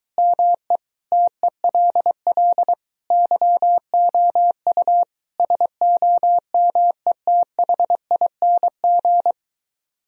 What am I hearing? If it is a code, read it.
Morse code, heard as a keyed tone, about 23 wpm, 700 Hz. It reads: ME TELL YOU SOMETHING